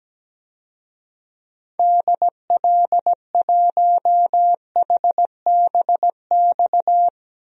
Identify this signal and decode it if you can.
Morse code, heard as a keyed tone, about 17 wpm, 700 Hz. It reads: DL1HBX
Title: DXCC-100 Call Signs - 6 Characters - Repeated 2x - 17wpm